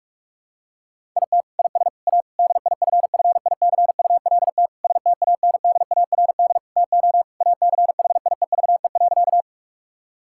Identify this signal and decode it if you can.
Morse code, heard as a keyed tone, about 46 wpm, 700 Hz. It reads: IT IS A DIFFICULT STANDARD TO ACHIEVE.
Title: Sentences from Top 1000 Words - Review Part 6 - 45wpm